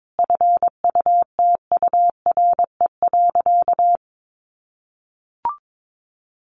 Morse code, heard as a keyed tone, about 22 wpm, 700 Hz. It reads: FUTURE.
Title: Sentences from Top 1200 Words 22wpm